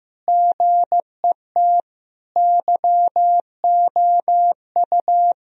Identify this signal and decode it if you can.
Morse code, heard as a keyed tone, about 15 wpm, 700 Hz. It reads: GET YOU
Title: Sentences from Top 1000 Words - Review Part 1 - 15wpm